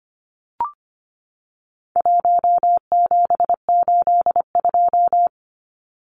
Morse code, heard as a keyed tone, about 25 wpm, 700 Hz. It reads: 1782
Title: Numbers - 4-Digit 25wpm